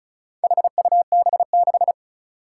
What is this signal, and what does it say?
Morse code, heard as a keyed tone, about 36 wpm, 700 Hz. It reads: HUB6
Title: Four Challenging Letter-Number Combinations 35wpm